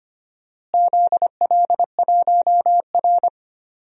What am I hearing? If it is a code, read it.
Morse code, heard as a keyed tone, about 25 wpm, 700 Hz. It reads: ZL1R
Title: DXCC-100 Call Signs - 4 Characters - Repeated 3x - 25wpm